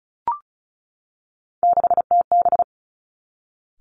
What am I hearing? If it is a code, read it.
Morse code, heard as a keyed tone, about 36 wpm, 700 Hz. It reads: 6TB
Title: Three Challenging Letter-Number Combinations 35wpm